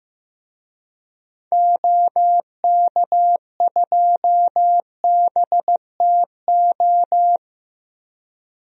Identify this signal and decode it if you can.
Morse code, heard as a keyed tone, about 15 wpm, 700 Hz. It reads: OK2BTO
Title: DXCC-100 Call Signs - 6 Characters - Repeated 2x - 15wpm